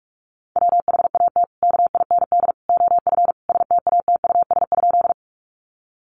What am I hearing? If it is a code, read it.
Morse code, heard as a keyed tone, about 44 wpm, 700 Hz. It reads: WHAT KIND OF STATUS?
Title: Sentences from Top 1400 Words 45wpm